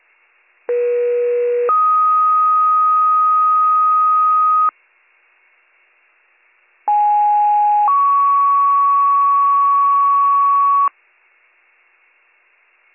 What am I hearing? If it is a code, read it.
Two-tone page, heard as a one-second tone, then a three-second tone, about 480 then 1250 Hz, about 830 then 1120 Hz. It is heard from 0.7 to 10.9 s.